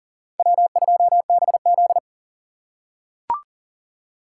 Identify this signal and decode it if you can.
Morse code, heard as a keyed tone, about 39 wpm, 700 Hz. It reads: W2BZ
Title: USA Call Signs - 4 Characters - 2x Character Spacing - 40wpm